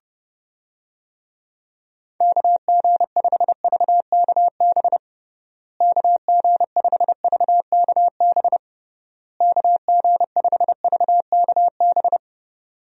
Morse code, heard as a keyed tone, about 30 wpm, 700 Hz. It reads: KG5VKB KG5VKB KG5VKB
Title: USA Call Signs - 6 Characters - Repeated 3x - 30wpm